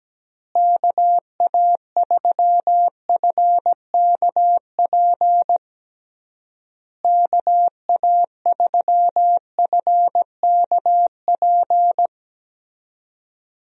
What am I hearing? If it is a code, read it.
Morse code, heard as a keyed tone, about 17 wpm, 700 Hz. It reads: KA3FKP KA3FKP
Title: USA Call Signs - 6 Characters - Repeated 3x - 17wpm